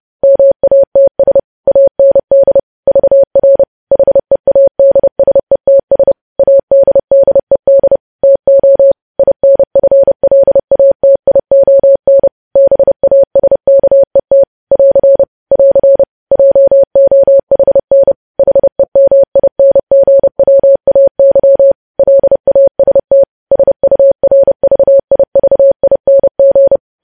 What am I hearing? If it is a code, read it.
Morse code, heard as a keyed tone, about 30 wpm, 550 Hz. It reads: MATS AND VR HEADSETS ADDED TO INFLATION BASKET <AR> <AR> JOHN HEMINGWAY LAST SURVIVING
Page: news headlines in morse code at 30 wpm